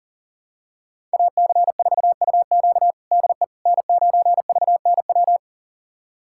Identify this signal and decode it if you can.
Morse code, heard as a keyed tone, about 40 wpm, 690 Hz. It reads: AC4UQ DE N9VNW
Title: USA Callsign DE Callsign - 5 Characters - 40wpm